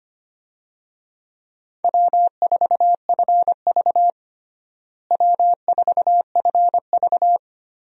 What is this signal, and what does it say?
Morse code, heard as a keyed tone, about 25 wpm, 700 Hz. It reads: W4FV W4FV
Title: USA Call Signs - 4 Characters - Repeated 2x - 25wpm